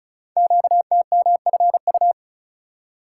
Morse code, heard as a keyed tone, about 36 wpm, 690 Hz. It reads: QTMFU
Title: Code Groups - 5 Letters - 35wpm